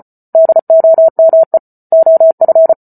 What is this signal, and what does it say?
Morse code, heard as a keyed tone, about 34 wpm, 650 Hz. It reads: DOME OF